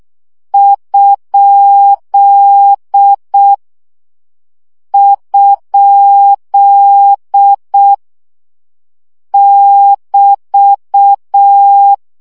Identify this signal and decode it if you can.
Morse code, heard as a keyed tone, about 6 wpm, 800 Hz. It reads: ? ? <BT>
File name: CW gyakorlas kerdojel 5 baud